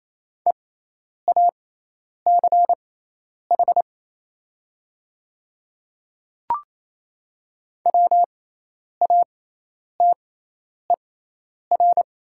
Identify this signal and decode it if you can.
Morse code, heard as a keyed tone, about 28 wpm, spaced as for 10 wpm, 700 Hz. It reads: EACH WATER
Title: 6x Character Spacing - Top 100 Words - 28wpm